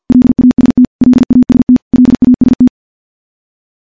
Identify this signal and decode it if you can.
Morse code, heard as a keyed tone, about 51 wpm, 250 Hz. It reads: LAST LAST LAST
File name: morse / morse-training-50WPM